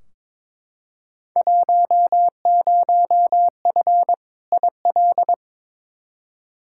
Morse code, heard as a keyed tone, about 22 wpm, 700 Hz. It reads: 10F IL
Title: Field Day Exchanges Encore - 22wpm